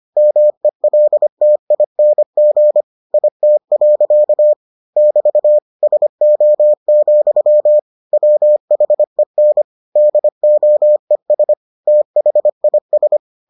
Morse code, heard as a keyed tone, about 25 wpm, 600 Hz. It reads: MELTING IT. <BT> SO, WHEN DOES THIS